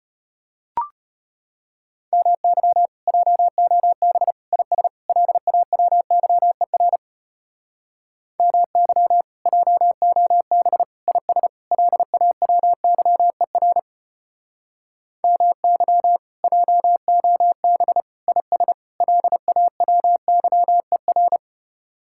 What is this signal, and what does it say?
Morse code, heard as a keyed tone, about 32 wpm, 700 Hz. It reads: MY JOB IS LAWYER MY JOB IS LAWYER MY JOB IS LAWYER